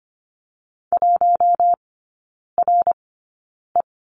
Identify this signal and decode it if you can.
Morse code, heard as a keyed tone, about 25 wpm, spaced as for 9 wpm, 700 Hz. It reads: 1RE
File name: course / 25:10wpm / 093 Lesson 16e - Letter c - Three Characters 25wpm